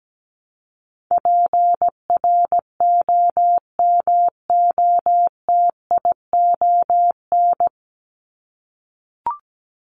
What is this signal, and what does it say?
Morse code, heard as a keyed tone, about 17 wpm, 700 Hz. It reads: PROMOTION
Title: Top 1501-1750 Words 17wpm